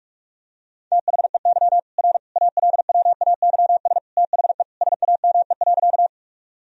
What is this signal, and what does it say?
Morse code, heard as a keyed tone, about 41 wpm, 700 Hz. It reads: THEY R ALWAYS THE SAME.
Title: Sentences from Top 300 Words and Abbreviations 45wpm